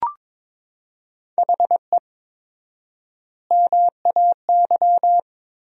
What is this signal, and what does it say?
Morse code, heard as a keyed tone, about 22 wpm, 700 Hz. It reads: HE MAY
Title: 4x Word-Spacing - Sentences from Top 300 Words - 22wpm